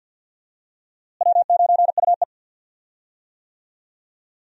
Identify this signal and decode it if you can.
Morse code, heard as a keyed tone, about 52 wpm, 700 Hz. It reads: W9UE
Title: USA Call Signs - 4 Characters - Repeated 3x - 50wpm